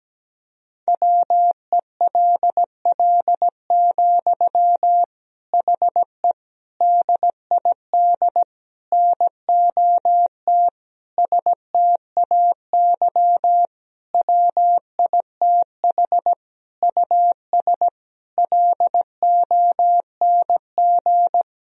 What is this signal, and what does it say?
Morse code, heard as a keyed tone, about 17 wpm, 700 Hz. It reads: WELL, HE DID NOT STAY WITH US LONG
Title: Sentences from Top 1000 Words - Review Part 2 - 17wpm